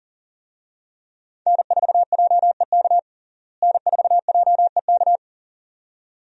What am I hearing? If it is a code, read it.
Morse code, heard as a keyed tone, about 40 wpm, 700 Hz. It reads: N4JEK N4JEK
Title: USA Call Signs - 5 Characters - Repeated 2x - 40wpm